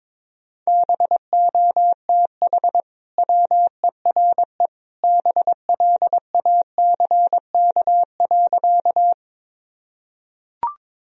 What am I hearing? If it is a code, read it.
Morse code, heard as a keyed tone, about 21 wpm, 700 Hz. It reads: BOTH WERE BLACK.
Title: Sentences from Top 1000 Words - Review Part 2 - 22wpm